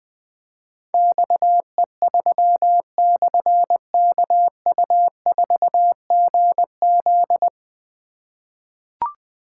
Morse code, heard as a keyed tone, about 20 wpm, 700 Hz. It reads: XE3/KU4GZ
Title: Difficult Call Signs 20wpm